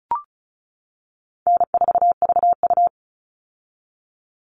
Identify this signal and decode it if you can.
Morse code, heard as a keyed tone, about 34 wpm, 700 Hz. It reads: N4VU
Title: All Cut Numbers - 4-Digit 35wpm